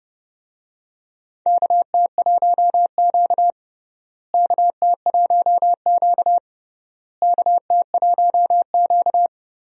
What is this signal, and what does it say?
Morse code, heard as a keyed tone, about 30 wpm, 700 Hz. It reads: KT1Q KT1Q KT1Q
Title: USA Call Signs - 4 Characters - Repeated 3x - 30wpm